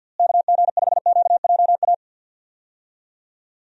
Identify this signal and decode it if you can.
Morse code, heard as a keyed tone, about 52 wpm, 700 Hz. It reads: KG5QJA
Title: USA Call Signs - 6 Characters - Repeated 3x - 50wpm